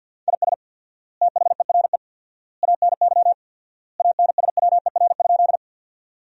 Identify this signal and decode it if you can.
Morse code, heard as a keyed tone, about 50 wpm, 700 Hz. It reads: IS THERE ANY ANSWER?
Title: 4x Word-Spacing - Sentences from Top 300 Words - 50wpm